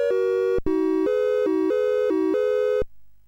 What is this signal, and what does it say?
There is no code.